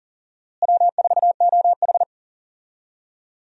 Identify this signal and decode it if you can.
Morse code, heard as a keyed tone, about 39 wpm, 700 Hz. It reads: W4OH
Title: USA Call Signs - 4 Characters - Repeated 3x - 40wpm